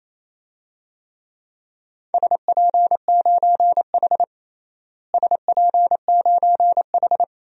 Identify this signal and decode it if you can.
Morse code, heard as a keyed tone, about 28 wpm, 700 Hz. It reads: SP9H SP9H
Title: DXCC-100 Call Signs - 4 Characters - Repeated 3x - 28wpm